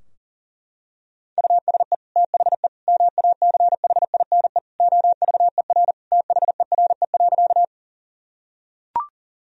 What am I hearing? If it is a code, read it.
Morse code, heard as a keyed tone, about 40 wpm, 700 Hz. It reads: USE THE MACHINE OVER THERE.
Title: Sentences from Top 1000 Words - Review Part 2 - 40wpm